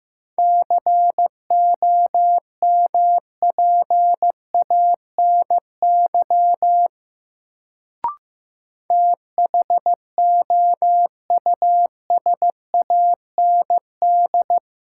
Morse code, heard as a keyed tone, about 15 wpm, 700 Hz. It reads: COMPANY THOUSAND
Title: Sentences from Top 600 Words - New Word Drill - 15wpm